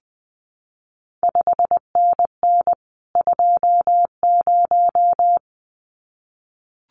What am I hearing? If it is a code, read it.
Morse code, heard as a keyed tone, about 19 wpm, 700 Hz. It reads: 5NN 20
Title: CQ WW DX Contest Exchange - 20wpm